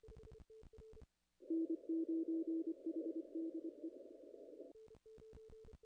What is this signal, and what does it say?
Morse code, heard as a keyed tone, about 25 wpm, 330 Hz. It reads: N9HDE
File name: contests / ss-cw-k5tr-op-n5rz -0100